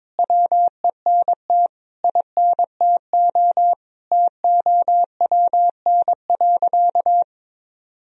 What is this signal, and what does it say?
Morse code, heard as a keyed tone, about 22 wpm, 700 Hz. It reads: WENT INTO TOWN.
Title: Sentences from Top 1000 Words - Review Part 1 - 22wpm